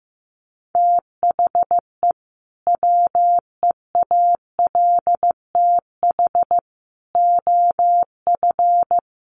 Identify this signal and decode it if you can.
Morse code, heard as a keyed tone, about 15 wpm, 700 Hz. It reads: THE WEALTH OF
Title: Sentences from Top 1200 Words 15wpm